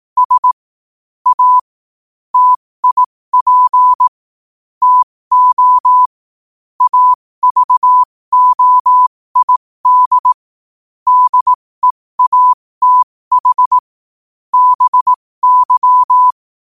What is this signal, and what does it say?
Morse code, heard as a keyed tone, about 18 wpm, spaced as for 15 wpm, 1000 Hz. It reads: S A TIP TO AVOID DEATH BY
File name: qotdfiles / 15wpm